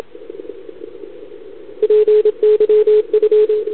Morse code, heard as a keyed tone, about 27 wpm, 430 Hz, with radio noise. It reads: PYF
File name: CQWWCW2021 / PY2PT